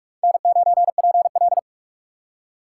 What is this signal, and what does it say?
Morse code, heard as a keyed tone, about 46 wpm, 700 Hz. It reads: N9PL
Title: USA Call Signs - 4 Characters - 3x Character Spacing - 45wpm